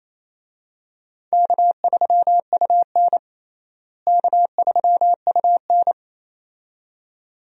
Morse code, heard as a keyed tone, about 27 wpm, 700 Hz. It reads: K3UN K3UN
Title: USA Call Signs - 4 Characters - Repeated 2x - 28wpm